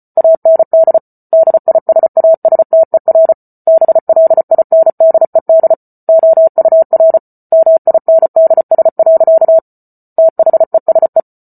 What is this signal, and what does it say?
Morse code, heard as a keyed tone, about 34 wpm, 650 Hz. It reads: AND DISASTER BLINDED OUR MINDS. THESE
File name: Book Two / 08-Chapter 8 - 34 WPM